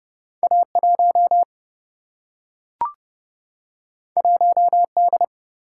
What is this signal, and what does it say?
Morse code, heard as a keyed tone, about 30 wpm, 700 Hz. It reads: A1 1D
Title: Single-Letter Single-Number 30wpm